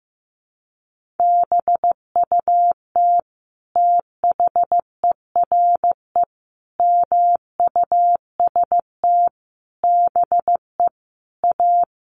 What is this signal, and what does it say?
Morse code, heard as a keyed tone, about 15 wpm, 700 Hz. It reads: BUT THERE MUST BE A
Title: Sentences from Top 1500 Words 15wpm